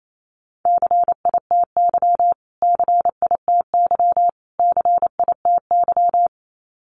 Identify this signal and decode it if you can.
Morse code, heard as a keyed tone, about 28 wpm, 700 Hz. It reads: CITY CITY CITY
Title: Top 300 Words Repeated 28wpm